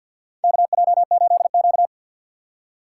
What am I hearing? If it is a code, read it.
Morse code, heard as a keyed tone, about 53 wpm, 700 Hz. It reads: KJ8Q
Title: USA Call Signs - 4 Characters - 6x Character Spacing - 50wpm